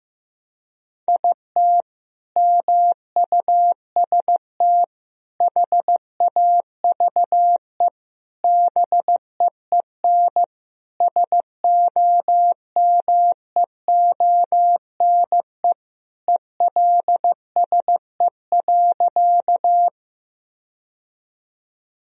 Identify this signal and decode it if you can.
Morse code, heard as a keyed tone, about 15 wpm, 700 Hz. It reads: IT MUST HAVE BEEN SOMEONE ELSE.
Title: Sentences from Top 1000 Words - Review Part 4 - 15wpm